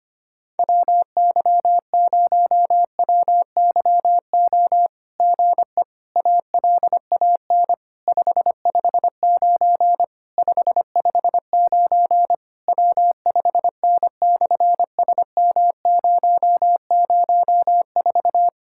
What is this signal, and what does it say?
Morse code, heard as a keyed tone, about 24 wpm, 700 Hz. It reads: WY0WYO GE ALAN 559 559 W5N/SM004
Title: SOTA QSO Practice 25wpm